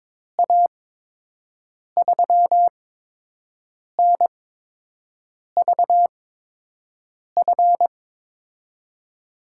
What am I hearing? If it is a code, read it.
Morse code, heard as a keyed tone, about 21 wpm, spaced as for 6 wpm, 700 Hz. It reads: A3NVF
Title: USA Call Signs - 6 Characters - 8x Character Spacing - 22wpm